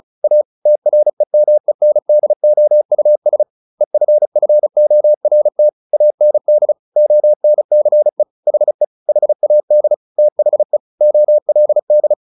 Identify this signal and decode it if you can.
Morse code, heard as a keyed tone, about 34 wpm, 600 Hz. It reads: A TREMENDOUS EFFORT AND ONCE HE HAD THE OLD
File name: the old man and the sea 2 35wpm